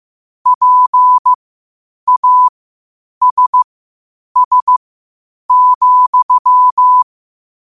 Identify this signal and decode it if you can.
Morse code, heard as a keyed tone, about 15 wpm, spaced as for 8 wpm, 1000 Hz. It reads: PASS,